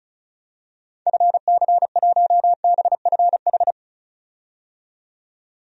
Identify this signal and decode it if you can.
Morse code, heard as a keyed tone, about 36 wpm, 700 Hz. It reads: FC1BFH
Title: DXCC-100 Call Signs - 6 Characters - 35wpm